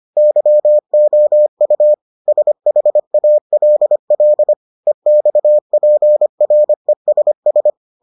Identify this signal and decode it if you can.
Morse code, heard as a keyed tone, about 25 wpm, 600 Hz. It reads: YOU SHALL EXPRESS